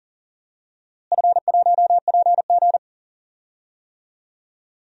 Morse code, heard as a keyed tone, about 40 wpm, 700 Hz. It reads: F1PG
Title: DXCC-100 Call Signs - 4 Characters - Repeated 2x - 40wpm